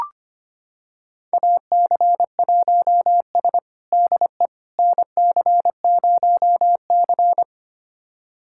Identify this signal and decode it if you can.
Morse code, heard as a keyed tone, about 25 wpm, 700 Hz. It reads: AC1S DE NC0C
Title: USA Callsign DE Callsign - 4 Characters - 25wpm